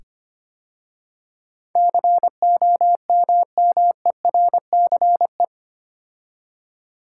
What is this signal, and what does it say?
Morse code, heard as a keyed tone, about 25 wpm, 700 Hz. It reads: COMMERCE
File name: 8 Letter Words - First 3 Letters Given - 25wpm